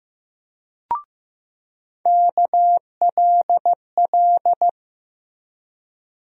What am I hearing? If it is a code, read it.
Morse code, heard as a keyed tone, about 15 wpm, 700 Hz. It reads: KLL
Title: USA Call Sign Postfixes - 3 Letters 15wpm